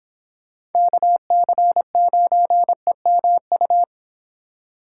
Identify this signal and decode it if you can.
Morse code, heard as a keyed tone, about 29 wpm, 700 Hz. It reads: KC9EMU